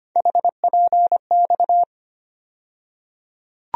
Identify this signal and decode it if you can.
Morse code, heard as a keyed tone, about 25 wpm, 700 Hz. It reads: HPX
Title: USA Call Sign Postfixes - 3 Letters 25wpm